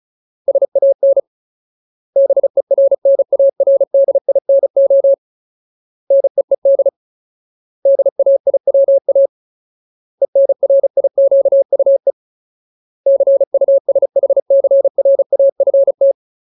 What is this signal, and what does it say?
Morse code, heard as a keyed tone, about 34 wpm, 550 Hz. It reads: SAN BERNARDINO NEED DAIWA ENRIQUE CUSHCRAFT
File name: Ultimate Word List / 35WPM Ultimate